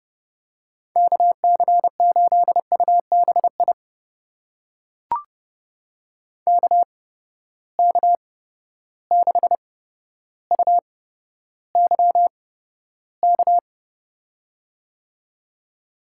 Morse code, heard as a keyed tone, about 32 wpm, spaced as for 13 wpm, 700 Hz. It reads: KC8UBI KK6UYK
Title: USA Call Signs - 6 Characters - 8x Character Spacing - 30wpm